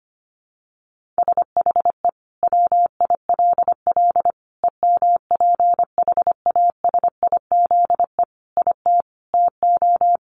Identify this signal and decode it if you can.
Morse code, heard as a keyed tone, about 25 wpm, 700 Hz. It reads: SHE WILL EMPHASIZE IT TO
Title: Sentences from Top 1500 Words 25wpm